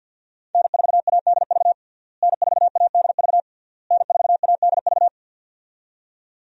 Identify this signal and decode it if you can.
Morse code, heard as a keyed tone, about 46 wpm, 700 Hz. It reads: N4ADV N4ADV N4ADV
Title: USA Call Signs - 5 Characters - Repeated 3x - 50wpm